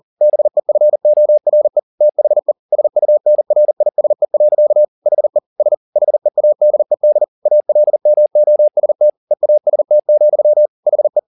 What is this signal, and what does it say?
Morse code, heard as a keyed tone, about 40 wpm, 600 Hz. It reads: BEFORE THE SUNRISE. HE S HEADED ALMOST EAST, HE
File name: the old man and the sea 2 40wpm